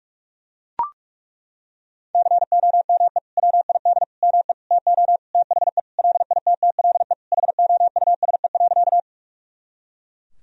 Morse code, heard as a keyed tone, about 42 wpm, 700 Hz. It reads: COME WID ME TO THE LITTLE HOUSE.
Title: Sentences from Top 200 Words and Abbreviations 45wpm